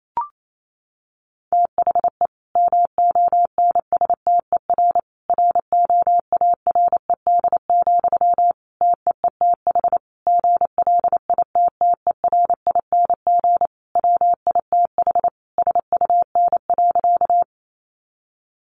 Morse code, heard as a keyed tone, about 28 wpm, 700 Hz. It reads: THE MONSTER ROARED, TEETH GLITTERING WITH SUN.
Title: A Sound Of Thunder 28wpm